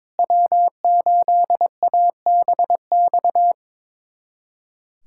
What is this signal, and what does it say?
Morse code, heard as a keyed tone, about 21 wpm, 700 Hz. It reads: W8ABX
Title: USA Callsign DE Callsign 22wpm